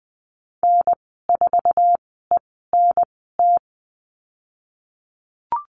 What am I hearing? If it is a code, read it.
Morse code, heard as a keyed tone, about 20 wpm, spaced as for 14 wpm, 700 Hz. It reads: N4ENT
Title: 048 Lesson 9g - Number 4 - Call Signs 20wpm